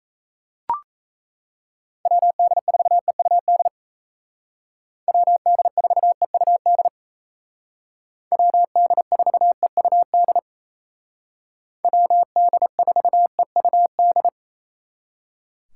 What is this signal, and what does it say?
Morse code, heard as a keyed tone, about 37 wpm, 700 Hz. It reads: WD4EUD WD4EUD WD4EUD WD4EUD